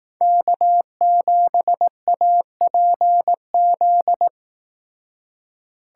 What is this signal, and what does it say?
Morse code, heard as a keyed tone, about 18 wpm, 700 Hz. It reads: K7APZ
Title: Speed Racing - USA 1x3 Call Signs 15wpm